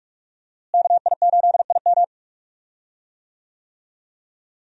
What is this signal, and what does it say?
Morse code, heard as a keyed tone, about 46 wpm, 700 Hz. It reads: KI8IM
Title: USA Call Signs - 5 Characters - Repeated 2x - 45wpm